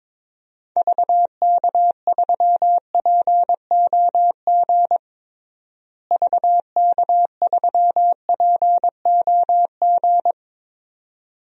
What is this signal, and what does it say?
Morse code, heard as a keyed tone, about 22 wpm, 700 Hz. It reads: VK3POG VK3POG
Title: DXCC-100 Call Signs - 6 Characters - Repeated 2x - 22wpm